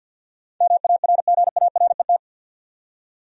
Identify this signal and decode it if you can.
Morse code, heard as a keyed tone, about 52 wpm, 700 Hz. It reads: MARGARET
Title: USA Common Names - 50wpm